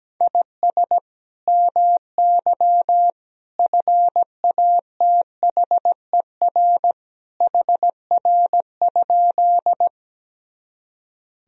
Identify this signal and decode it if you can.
Morse code, heard as a keyed tone, about 17 wpm, 700 Hz. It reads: IS MY FATHER HR?
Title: Sentences from Top 300 Words and Abbreviations 17wpm